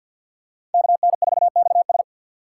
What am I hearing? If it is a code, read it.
Morse code, heard as a keyed tone, about 47 wpm, 700 Hz. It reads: KN4XS
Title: USA Call Signs - 5 Characters - 3x Character Spacing - 50wpm